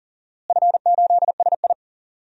Morse code, heard as a keyed tone, about 39 wpm, 700 Hz. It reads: F8SI
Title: DXCC-100 Call Signs - 4 Characters - Repeated 2x - 40wpm